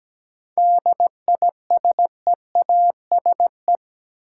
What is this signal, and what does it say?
Morse code, heard as a keyed tone, about 17 wpm, 700 Hz. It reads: DISEASE
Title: Sentences from Top 1000 Words - Review Part 6 - New Word Drill - 17wpm